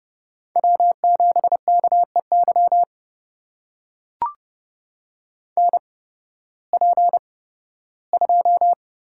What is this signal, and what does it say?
Morse code, heard as a keyed tone, about 30 wpm, spaced as for 16 wpm, 700 Hz. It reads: W7KEY NP2
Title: USA Call Signs - 5 Characters - 8x Character Spacing - 30wpm